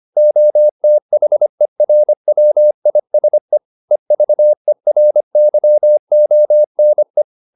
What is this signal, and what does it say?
Morse code, heard as a keyed tone, about 25 wpm, 600 Hz. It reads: OTHERWISE EVERYONE